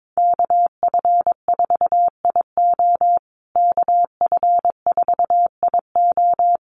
Morse code, heard as a keyed tone, about 22 wpm, 700 Hz. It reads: KF4IO KF4IO
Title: USA Call Signs Repeated 2x 22wpm